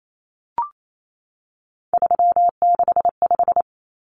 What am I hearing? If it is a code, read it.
Morse code, heard as a keyed tone, about 28 wpm, 700 Hz. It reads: 365
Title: Common Cut Numbers - 3-Digit 28wpm